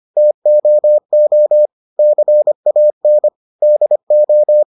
Morse code, heard as a keyed tone, about 25 wpm, 600 Hz. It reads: TOO CAN DO